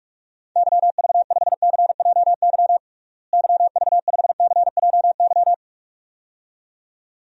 Morse code, heard as a keyed tone, about 46 wpm, 700 Hz. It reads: YV5CJY YV5CJY
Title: DXCC-100 Call Signs - 6 Characters - Repeated 2x - 45wpm